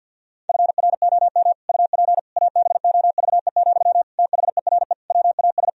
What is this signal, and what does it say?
Morse code, heard as a keyed tone, about 48 wpm, 700 Hz. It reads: FROM UP ABOVE, THERE WAS
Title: Sentences from Top 1000 Words - Review Part 2 - 50wpm